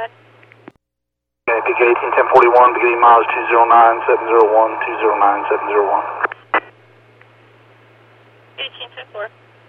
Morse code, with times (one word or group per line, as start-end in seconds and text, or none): none